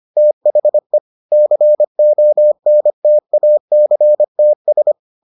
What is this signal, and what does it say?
Morse code, heard as a keyed tone, about 25 wpm, 600 Hz. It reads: THE CONTACTS